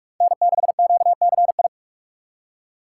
Morse code, heard as a keyed tone, about 46 wpm, 700 Hz. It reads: N6QCI
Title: USA Call Signs - 5 Characters - 4x Character Spacing - 45wpm